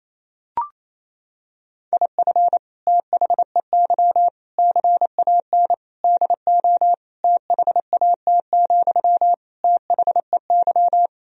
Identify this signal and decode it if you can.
Morse code, heard as a keyed tone, about 28 wpm, 700 Hz. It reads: IF THEY CAN DO THAT, THEY
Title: Sentences from Top 1000 Words - Review Part 4 - 28wpm